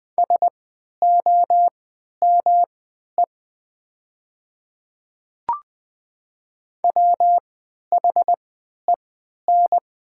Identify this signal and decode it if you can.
Morse code, heard as a keyed tone, about 20 wpm, spaced as for 11 wpm, 700 Hz. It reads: SOME WHEN
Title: 3x Character Spacing - Top 200 Words - 20wpm